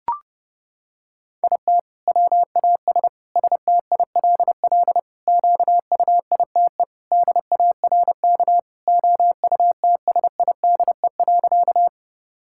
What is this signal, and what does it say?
Morse code, heard as a keyed tone, about 30 wpm, 700 Hz. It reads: IT WAS STILL QUITE DARK OUTSIDE.
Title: Sentences from Top 1000 Words - Review Part 6 - 30wpm